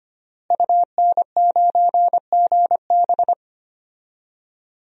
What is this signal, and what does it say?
Morse code, heard as a keyed tone, about 25 wpm, 700 Hz. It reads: UN9GB
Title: DXCC-100 Call Signs - 5 Characters - Repeated 2x - 25wpm